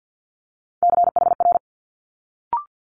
Morse code, heard as a keyed tone, about 50 wpm, 700 Hz. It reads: CHR
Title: Multi-letter Phonemes Quick Fire 50wpm